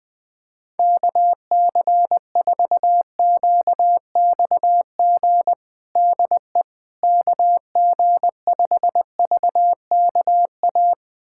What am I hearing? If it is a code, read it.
Morse code, heard as a keyed tone, about 20 wpm, 700 Hz. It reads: KC4QXG DE KG5VKA
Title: USA Callsign DE Callsign - 6 Characters - 20wpm